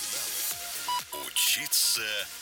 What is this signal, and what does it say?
Speech. There is no code.